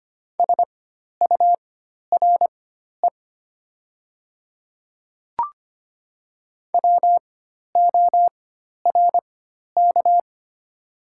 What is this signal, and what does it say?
Morse code, heard as a keyed tone, about 25 wpm, spaced as for 12 wpm, 700 Hz. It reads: SURE WORK
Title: 4x Character Spacing - Top 300 Words - 25wpm